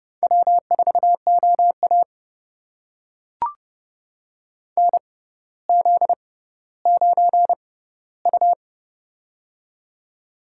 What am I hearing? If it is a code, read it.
Morse code, heard as a keyed tone, about 30 wpm, spaced as for 15 wpm, 700 Hz. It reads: W4OA NZ9U
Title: USA Call Signs - 4 Characters - 6x Character Spacing - 30wpm